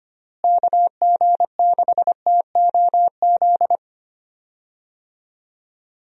Morse code, heard as a keyed tone, about 25 wpm, 700 Hz. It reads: KG6TOZ
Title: USA Call Signs - 6 Characters - Repeated 2x - 25wpm